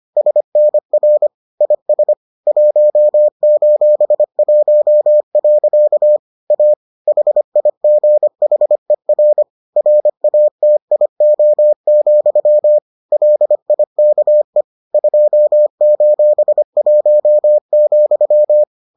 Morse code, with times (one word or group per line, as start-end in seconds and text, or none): 0.2-1.3 SNR
1.6-2.1 IS
2.5-6.2 1:1.
6.5-6.7 A
7.1-9.4 HIGHER
9.8-12.8 RATIO,
13.1-14.6 LIKE
14.9-18.6 2:1,